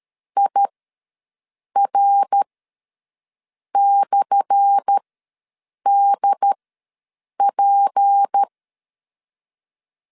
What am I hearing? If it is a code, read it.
Morse code, heard as a keyed tone, about 13 wpm, spaced as for 6 wpm, 800 Hz. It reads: IR/DP
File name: LEZIONI / 10 SIGNS / 10C DOT ws 12wpm 3548